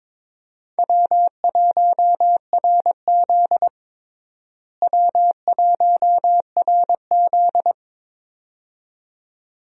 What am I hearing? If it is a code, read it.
Morse code, heard as a keyed tone, about 21 wpm, 700 Hz. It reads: W1RZ W1RZ
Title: USA Call Signs - 4 Characters - Repeated 2x - 22wpm